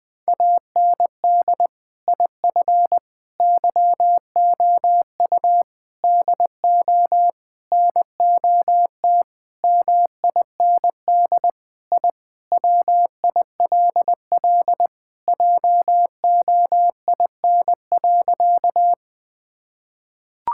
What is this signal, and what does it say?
Morse code, heard as a keyed tone, about 21 wpm, 700 Hz. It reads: AND IF YOU DO NOT MIND I WILL JOIN.
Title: Sentences with 5-Word Phrases - Spoken Beginning - 20wpm